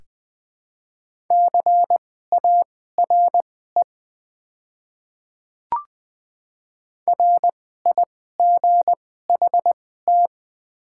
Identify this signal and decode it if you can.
Morse code, heard as a keyed tone, about 20 wpm, spaced as for 14 wpm, 700 Hz. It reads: CARE RIGHT
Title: 2x Character Spacing - Top 300 Words - 20wpm